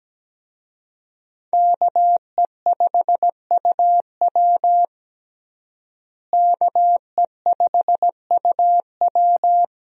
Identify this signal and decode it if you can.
Morse code, heard as a keyed tone, about 17 wpm, 700 Hz. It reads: KE5UW KE5UW
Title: USA Call Signs - 5 Characters - Repeated 2x - 17wpm